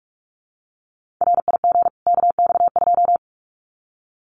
Morse code, heard as a keyed tone, about 44 wpm, 700 Hz. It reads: RIG KX2
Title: Common QSO Phrases 45wpm